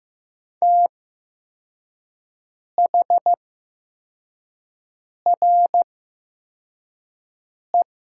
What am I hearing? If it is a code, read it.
Morse code, heard as a keyed tone, about 15 wpm, spaced as for 4 wpm, 700 Hz. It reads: THRE
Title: 8x Character Spacing - Top 300 Words - 15wpm